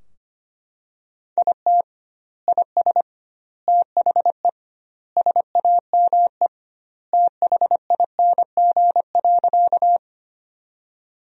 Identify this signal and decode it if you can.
Morse code, heard as a keyed tone, about 25 wpm, 700 Hz. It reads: IT IS THE SAME THING.
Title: 2x Word-Spacing - Sentences from Top 200 Words - 25wpm